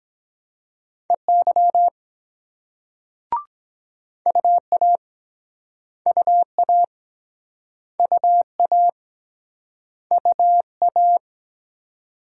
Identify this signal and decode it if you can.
Morse code, heard as a keyed tone, about 22 wpm, 700 Hz. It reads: EY UA UA UA UA